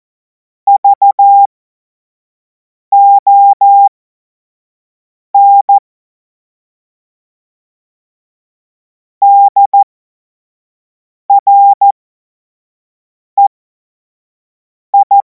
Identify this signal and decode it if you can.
Morse code, heard as a keyed tone, about 14 wpm, spaced as for 5 wpm, 800 Hz. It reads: VON DREI
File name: Morsekurs / Witze / Witz 09. Juni 2026 25 F70